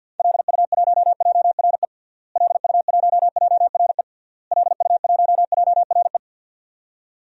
Morse code, heard as a keyed tone, about 50 wpm, 700 Hz. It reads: LU1JRE LU1JRE LU1JRE
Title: DXCC-100 Call Signs - 6 Characters - Repeated 3x - 50wpm